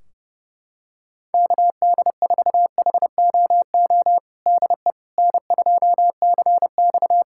Morse code, heard as a keyed tone, about 30 wpm, 700 Hz. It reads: KD4HOO DE N2CX
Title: Practice QSO - Exchange 4 - Ending - 30wpm